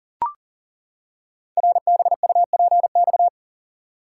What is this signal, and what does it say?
Morse code, heard as a keyed tone, about 42 wpm, 700 Hz. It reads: RBUPX
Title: Code Groups - 5 Letters - 40wpm